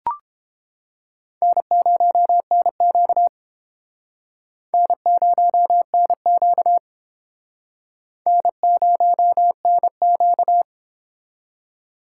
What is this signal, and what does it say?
Morse code, heard as a keyed tone, about 29 wpm, 700 Hz. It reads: N0NQ N0NQ N0NQ